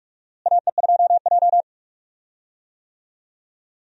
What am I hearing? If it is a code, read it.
Morse code, heard as a keyed tone, about 46 wpm, 700 Hz. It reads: AE2J
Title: USA Call Signs - 4 Characters - Repeated 3x - 45wpm